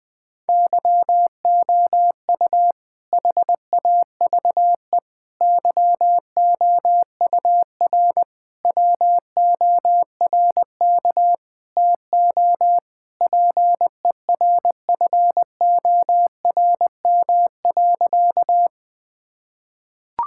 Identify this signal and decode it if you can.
Morse code, heard as a keyed tone, about 20 wpm, 700 Hz. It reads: YOU HAVE YOUR WORK TO PERFORM.
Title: Sentences from Top 1000 Words - Review Part 5 - 20wpm